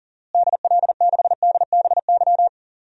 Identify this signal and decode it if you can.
Morse code, heard as a keyed tone, about 40 wpm, 700 Hz. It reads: DL6DBY
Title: DXCC-100 Call Signs - 6 Characters - Repeated 3x - 40wpm